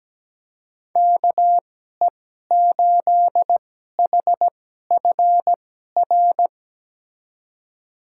Morse code, heard as a keyed tone, about 17 wpm, spaced as for 12 wpm, 700 Hz. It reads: KE8HFR
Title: USA Call Signs - 6 Characters - 2x Character Spacing - 17wpm